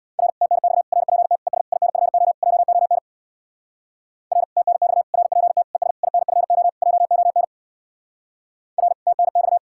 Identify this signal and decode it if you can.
Morse code, heard as a keyed tone, about 59 wpm, 700 Hz. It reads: R TNX RPT ES INFO JON R TNX RPT ES INFO JON R TNX
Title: Speed-Racing Common QSO Phrases - 50wpm